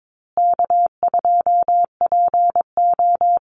Morse code, heard as a keyed tone, about 22 wpm, 700 Hz. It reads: K2PO
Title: Easy Call Signs Repeated 3x - 22wpm